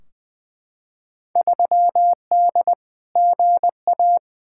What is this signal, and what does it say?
Morse code, heard as a keyed tone, about 20 wpm, 700 Hz. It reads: 3D GA
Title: Field Day Exchanges Encore - 20wpm